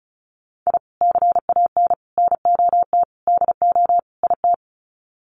Morse code, heard as a keyed tone, about 34 wpm, 700 Hz. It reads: I CAN NOT DO IT
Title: Sets of 5 Words Encore 35wpm